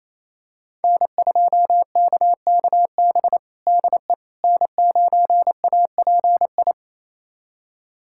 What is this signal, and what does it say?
Morse code, heard as a keyed tone, about 28 wpm, 700 Hz. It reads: N2KKB DE N9API
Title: USA Callsign DE Callsign - 5 Characters - 28wpm